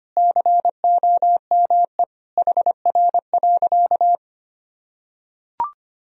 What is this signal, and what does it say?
Morse code, heard as a keyed tone, about 25 wpm, 700 Hz. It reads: COME HR.
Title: Sentences from Top 300 Words and Abbreviations 25wpm